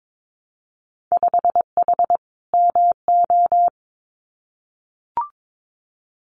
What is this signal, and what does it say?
Morse code, heard as a keyed tone, about 22 wpm, 700 Hz. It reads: 5H MO
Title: Winter Field Day Exchanges Encore 22wpm